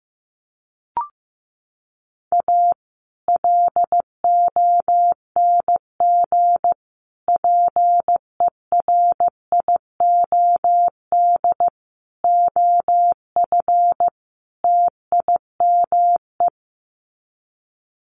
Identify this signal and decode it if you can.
Morse code, heard as a keyed tone, about 15 wpm, 700 Hz. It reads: A LONG PERIOD OF TIME
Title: Sets of 5 Words 15wpm